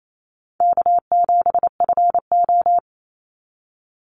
Morse code, heard as a keyed tone, about 28 wpm, 700 Hz. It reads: K7FO
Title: NA SOTA Chaser Callsigns 28wpm